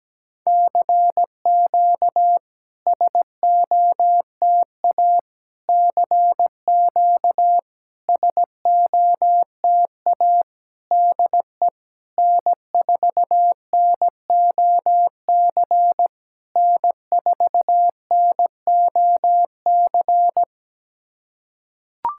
Morse code, heard as a keyed tone, about 17 wpm, 700 Hz. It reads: CQ SOTA CQ SOTA DE N4NOC N4NOC
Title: SOTA QSO Practice 17wpm